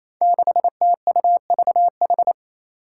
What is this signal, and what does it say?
Morse code, heard as a keyed tone, about 27 wpm, 700 Hz. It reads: 6TUVH
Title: Five Challenging Letter-Number Combinations 28wpm